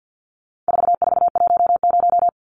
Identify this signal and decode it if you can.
Morse code, heard as a keyed tone, about 50 wpm, 700 Hz. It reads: 4410
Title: Numbers - 4-Digit 50wpm